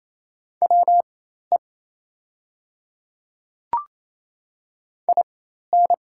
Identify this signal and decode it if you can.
Morse code, heard as a keyed tone, about 28 wpm, spaced as for 13 wpm, 700 Hz. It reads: WE IN
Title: 4x Character Spacing - Top 100 Words - 28wpm